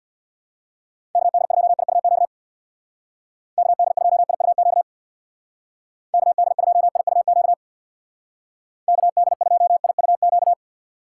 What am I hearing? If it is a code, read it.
Morse code, heard as a keyed tone, about 59 wpm, 700 Hz. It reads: KD2IUQ KD2IUQ KD2IUQ KD2IUQ
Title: Speed Racing - USA 2x3 Call Signs 50wpm